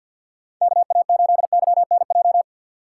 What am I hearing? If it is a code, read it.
Morse code, heard as a keyed tone, about 52 wpm, 700 Hz. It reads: KA8YNJ
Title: USA Call Signs - 3x Character Spacing - 50wpm